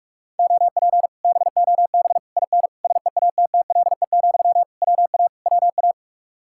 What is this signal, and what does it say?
Morse code, heard as a keyed tone, about 45 wpm, 700 Hz. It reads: OP BOB IN SEATTLE, WA WA